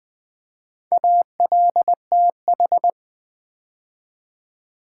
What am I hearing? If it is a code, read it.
Morse code, heard as a keyed tone, about 20 wpm, 700 Hz. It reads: ALTH
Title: Build A Word - 2 Letters At A Time - 8-Letter Words - 20wpm